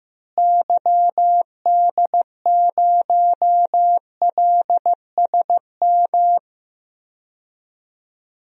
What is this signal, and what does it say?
Morse code, heard as a keyed tone, about 15 wpm, 700 Hz. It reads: YD0LSM
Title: DXCC-100 Call Signs - 6 Characters - Repeated 3x - 15wpm